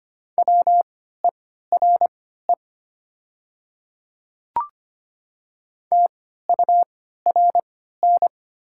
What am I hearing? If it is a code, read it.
Morse code, heard as a keyed tone, about 25 wpm, spaced as for 14 wpm, 690 Hz. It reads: WERE TURN
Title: 3x Character Spacing - Top 300 Words - 25wpm